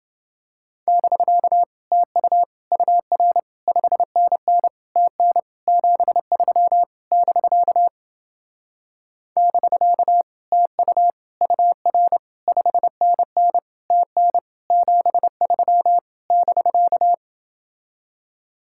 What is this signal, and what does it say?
Morse code, heard as a keyed tone, about 29 wpm, 700 Hz. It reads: <BK> TU UR 5NN TN 73 <BK> <BK> TU UR 5NN TN 73 <BK>